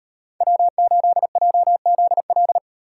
Morse code, heard as a keyed tone, about 38 wpm, 700 Hz. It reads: W8JZL